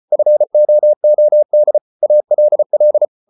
Morse code, heard as a keyed tone, about 34 wpm, 600 Hz. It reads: FOOD ALL